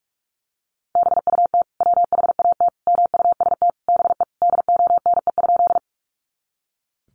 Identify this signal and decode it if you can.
Morse code, heard as a keyed tone, about 44 wpm, 700 Hz. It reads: BUT WHAT MUST BE DONE?
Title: Sentences from Top 500 Words 45wpm